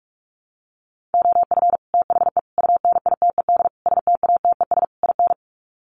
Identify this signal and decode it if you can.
Morse code, heard as a keyed tone, about 44 wpm, 700 Hz. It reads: OF THE UNITED STATES IN
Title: Sets of 5 Words Encore 45wpm